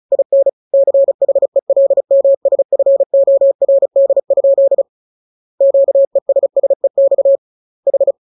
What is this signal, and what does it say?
Morse code, heard as a keyed tone, about 34 wpm, 550 Hz. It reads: IN CHELMSFORD? QESSEX H
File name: jokes / joke 25 35 0000